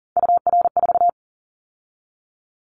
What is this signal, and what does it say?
Morse code, heard as a keyed tone, about 42 wpm, 700 Hz. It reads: UR4
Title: Callsign Prefixes for Top 25 DXCC - 40wpm